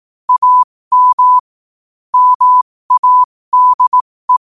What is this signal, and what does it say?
Morse code, heard as a keyed tone, about 18 wpm, spaced as for 15 wpm, 1000 Hz. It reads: AM MADE
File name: qotdfiles / 15wpm